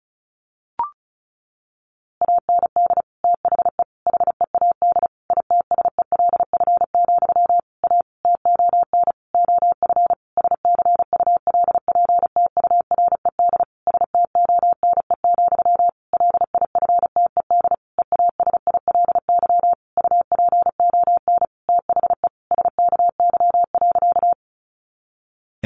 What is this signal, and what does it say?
Morse code, heard as a keyed tone, about 34 wpm, 700 Hz. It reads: AND THE HEAD ITSELF, A TON OF SCULPTURED STONE, LIFTED EASILY UPON THE SKY.
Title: A Sound Of Thunder 35wpm